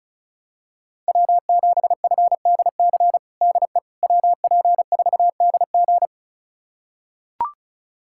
Morse code, heard as a keyed tone, about 34 wpm, 700 Hz. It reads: W7FDC DE WP4DG
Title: USA Callsign DE Callsign - 5 Characters - 35wpm